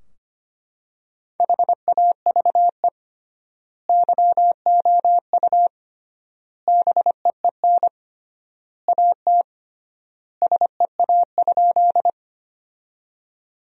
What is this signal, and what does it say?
Morse code, heard as a keyed tone, about 25 wpm, 700 Hz. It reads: HAVE YOU BEEN AT SEA?
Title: 3x Word-Spacing - Sentences from Top 300 Words - 25wpm